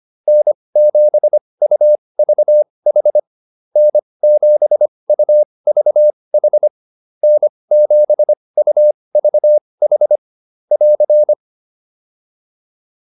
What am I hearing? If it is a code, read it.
Morse code, heard as a keyed tone, about 25 wpm, spaced as for 20 wpm, 600 Hz. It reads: N7UVH N7UVH N7UVH <AR>
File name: QSO105a 20